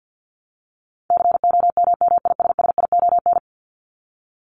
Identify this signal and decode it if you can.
Morse code, heard as a keyed tone, about 50 wpm, 700 Hz. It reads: COMMISSION
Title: Top 1251-1500 Words 50wpm